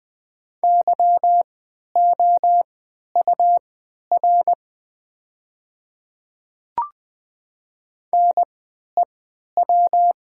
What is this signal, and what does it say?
Morse code, heard as a keyed tone, about 20 wpm, spaced as for 11 wpm, 700 Hz. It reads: YOUR NEW
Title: 3x Character Spacing - Top 300 Words - 20wpm